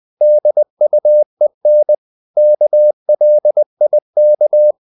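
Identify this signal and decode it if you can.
Morse code, heard as a keyed tone, about 20 wpm, 600 Hz. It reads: DUEN KLIK
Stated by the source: Morse code practice of short Danish words at 20 wpm, from CW med Gnister.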